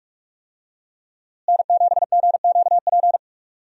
Morse code, heard as a keyed tone, about 46 wpm, 700 Hz. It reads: N7GQP
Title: USA Call Signs - 5 Characters - Repeated 3x - 45wpm